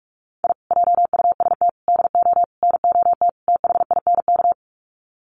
Morse code, heard as a keyed tone, about 44 wpm, 700 Hz. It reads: I JUST DO NOT THINK
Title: Sets of 5 Words Encore 45wpm